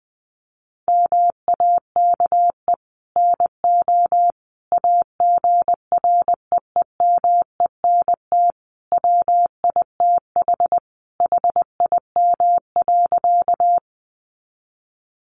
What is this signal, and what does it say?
Morse code, heard as a keyed tone, about 20 wpm, 700 Hz. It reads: MAKE NO AGREEMENT WITH HIM.
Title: Sentences from Top 1500 Words 20wpm